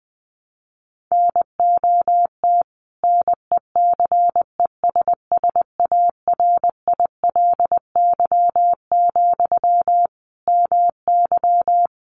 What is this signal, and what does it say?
Morse code, heard as a keyed tone, about 20 wpm, 700 Hz. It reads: NOT NECESSARILY, MY
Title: Sentences from Top 1400 Words 20wpm